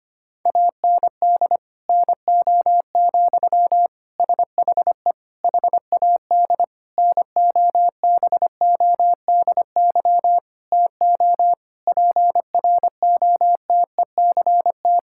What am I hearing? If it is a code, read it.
Morse code, heard as a keyed tone, about 25 wpm, 700 Hz. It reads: AND NO, SHE HAD NOBODY TO PROTECT
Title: Sentences from Top 1600 Words 25wpm